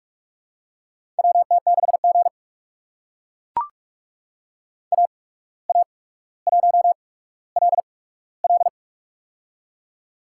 Morse code, heard as a keyed tone, about 44 wpm, spaced as for 18 wpm, 700 Hz. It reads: WT6G AA1LL
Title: USA Call Signs - 8x Character Spacing - 45wpm